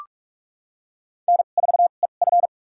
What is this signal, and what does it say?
Morse code, heard as a keyed tone, about 46 wpm, spaced as for 32 wpm, 700 Hz. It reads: N4EF
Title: USA Call Signs - 4 Characters - 2x Character Spacing - 45wpm